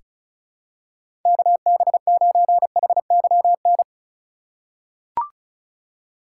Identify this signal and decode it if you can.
Morse code, heard as a keyed tone, about 34 wpm, 700 Hz. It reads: KB9HYN
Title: USA Call Signs - 4x Character Spacing - 35wpm